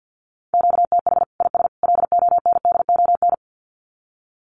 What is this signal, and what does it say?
Morse code, heard as a keyed tone, about 50 wpm, 700 Hz. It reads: QTH IS LONDON
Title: Common QSO Phrases 50wpm